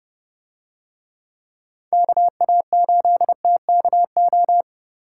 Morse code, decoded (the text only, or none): KA8TKO